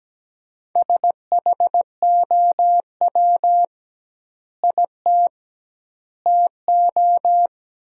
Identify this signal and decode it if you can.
Morse code, heard as a keyed tone, about 17 wpm, 700 Hz. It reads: SHOW IT TO
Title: 2x Word-Spacing - Sentences from Top 200 Words - 17wpm